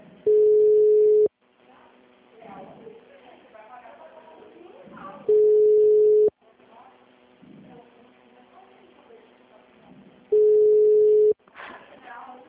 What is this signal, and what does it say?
Ringback tone (Europe, 425 Hz), with speech.
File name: out- -2009- - - .10088